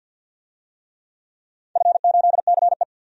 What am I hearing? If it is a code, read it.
Morse code, heard as a keyed tone, about 47 wpm, 700 Hz. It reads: F8CE
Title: DXCC-100 Call Signs - 4 Characters - 50wpm